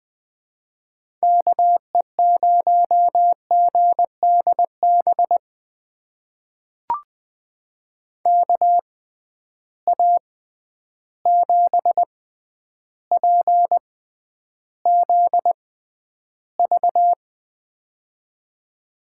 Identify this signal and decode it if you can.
Morse code, heard as a keyed tone, about 20 wpm, spaced as for 10 wpm, 700 Hz. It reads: KE0GDB KA7PZV
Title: USA Call Signs - 6 Characters - 6x Character Spacing - 20wpm